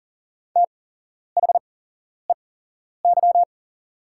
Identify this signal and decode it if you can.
Morse code, heard as a keyed tone, about 40 wpm, spaced as for 11 wpm, 690 Hz. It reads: THEY
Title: 8x Character Spacing - Top 100 Words - 40wpm